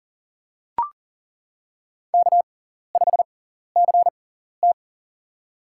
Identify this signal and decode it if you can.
Morse code, heard as a keyed tone, about 40 wpm, spaced as for 14 wpm, 700 Hz. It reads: K5CT
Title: USA Call Signs - 6x Character Spacing - 40wpm